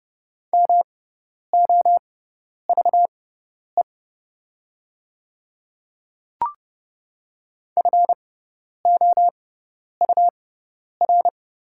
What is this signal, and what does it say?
Morse code, heard as a keyed tone, about 30 wpm, spaced as for 10 wpm, 700 Hz. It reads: MOVE FOUR
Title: 6x Character Spacing - Top 300 Words - 30wpm